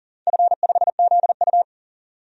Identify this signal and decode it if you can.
Morse code, heard as a keyed tone, about 42 wpm, 700 Hz. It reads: F5ZU
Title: DXCC-100 Call Signs - 4 Characters - 40wpm